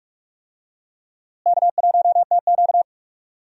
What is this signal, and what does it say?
Morse code, heard as a keyed tone, about 45 wpm, 700 Hz. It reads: K1TQ
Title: USA Call Signs - 4 Characters - Repeated 2x - 45wpm